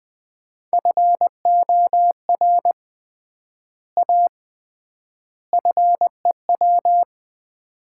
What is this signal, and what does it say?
Morse code, heard as a keyed tone, about 21 wpm, 700 Hz. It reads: FOR A FEW
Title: 3x Word-Spacing - Sets of 3 Words - 20wpm